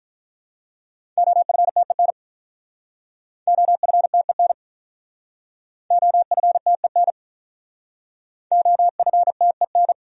Morse code, heard as a keyed tone, about 44 wpm, 700 Hz. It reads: OFTEN OFTEN OFTEN OFTEN